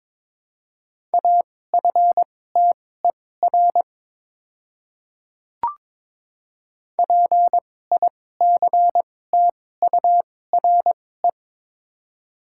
Morse code, heard as a keyed tone, about 22 wpm, spaced as for 16 wpm, 700 Hz. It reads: AFTER PICTURE
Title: 2x Character Spacing - Top 200 Words - 22wpm